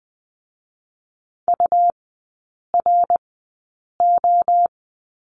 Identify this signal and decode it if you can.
Morse code, heard as a keyed tone, about 20 wpm, spaced as for 8 wpm, 700 Hz. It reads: URO
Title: 138 Lesson 23e - Letter f - Three Characters 20wpm